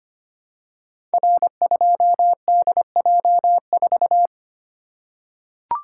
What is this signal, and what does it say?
Morse code, heard as a keyed tone, about 25 wpm, 700 Hz. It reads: R2DJ4
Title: Code Groups - 5 Letter:Numbers - 25wpm